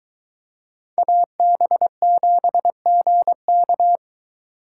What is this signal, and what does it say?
Morse code, heard as a keyed tone, about 23 wpm, 700 Hz. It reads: AB7GK